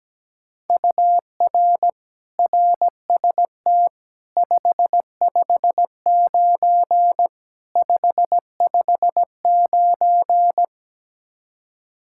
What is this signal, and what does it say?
Morse code, heard as a keyed tone, about 17 wpm, 700 Hz. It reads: UR RST 559 559
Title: Practice QSO - Exchange 1 - Beginning - 17wpm